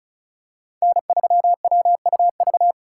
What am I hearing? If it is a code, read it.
Morse code, heard as a keyed tone, about 36 wpm, 700 Hz. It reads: N3WUV
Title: USA Call Signs - 5 Characters - 3x Character Spacing - 35wpm